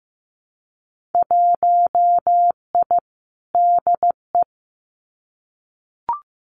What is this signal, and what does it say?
Morse code, heard as a keyed tone, about 15 wpm, 700 Hz. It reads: 1I DE
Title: Winter Field Day Exchanges Encore 15wpm